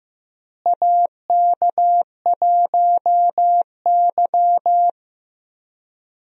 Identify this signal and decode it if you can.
Morse code, heard as a keyed tone, about 15 wpm, 700 Hz. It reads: AK1Y
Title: USA Call Signs - 4 Characters - Repeated 3x - 15wpm